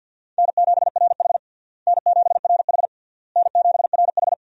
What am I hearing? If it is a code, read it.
Morse code, heard as a keyed tone, about 52 wpm, 700 Hz. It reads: N7RH N7RH N7RH
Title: USA Call Signs - 4 Characters - Repeated 3x - 50wpm